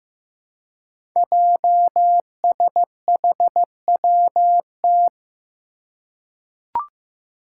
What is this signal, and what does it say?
Morse code, heard as a keyed tone, about 15 wpm, 700 Hz. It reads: JSHWT
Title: Five Challenging Letter-Number Combinations 15wpm